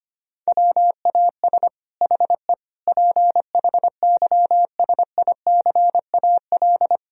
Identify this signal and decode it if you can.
Morse code, heard as a keyed tone, about 25 wpm, 700 Hz. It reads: WAS HE PHYSICAL
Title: Sentences from Top 1000 Words - Review Part 4 - 25wpm